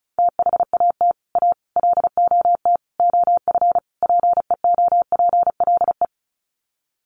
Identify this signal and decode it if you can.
Morse code, heard as a keyed tone, about 34 wpm, 700 Hz. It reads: THAT A LOT OF PEOPLE
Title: Sets of 5 Words 35wpm